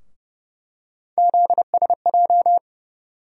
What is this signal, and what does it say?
Morse code, heard as a keyed tone, about 30 wpm, 700 Hz. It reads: ZSJ
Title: USA Call Sign Postfixes - 3 Letters 30wpm